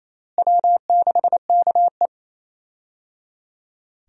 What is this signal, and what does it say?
Morse code, heard as a keyed tone, about 28 wpm, 700 Hz. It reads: W6KE
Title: USA Call Signs - 4 Characters - Repeated 3x - 28wpm